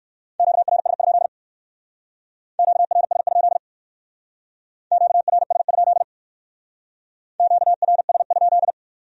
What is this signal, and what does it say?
Morse code, heard as a keyed tone, about 61 wpm, 700 Hz. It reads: QRS? QRS? QRS? QRS?